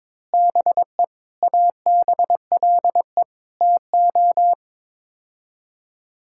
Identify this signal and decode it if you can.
Morse code, heard as a keyed tone, about 22 wpm, 700 Hz. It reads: BE ABLE TO
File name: practice / 22wpm / Build A Phrase - 5-Word Phrases - 22wpm